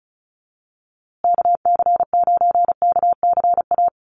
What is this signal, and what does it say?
Morse code, heard as a keyed tone, about 34 wpm, 700 Hz. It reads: KC9KCA
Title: USA Call Signs Repeated 3x 35wpm